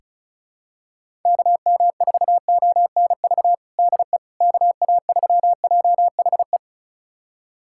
Morse code, heard as a keyed tone, about 36 wpm, 690 Hz. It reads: KM4ONV DE KA3JHE
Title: USA Callsign DE Callsign - 6 Characters - 35wpm